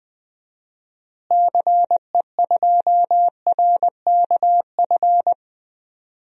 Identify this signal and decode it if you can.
Morse code, heard as a keyed tone, about 20 wpm, 700 Hz. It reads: CE2RKF
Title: DXCC-100 Call Signs - 6 Characters - Repeated 2x - 20wpm